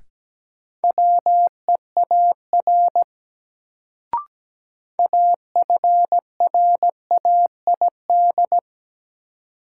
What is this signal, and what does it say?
Morse code, heard as a keyed tone, about 17 wpm, 700 Hz. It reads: WEAR AFRAID
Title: Sentences from Top 1500 Words - New Word Drill Alt - 17wpm